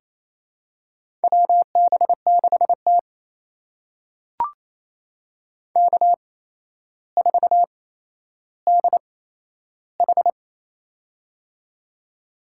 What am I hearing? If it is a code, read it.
Morse code, heard as a keyed tone, about 28 wpm, spaced as for 12 wpm, 700 Hz. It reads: WB6T K4DH
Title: USA Call Signs - 4 Characters - 8x Character Spacing - 28wpm